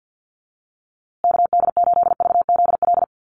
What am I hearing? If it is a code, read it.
Morse code, heard as a keyed tone, about 48 wpm, 700 Hz. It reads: KD8VZL
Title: USA Call Signs Repeated 3x 50wpm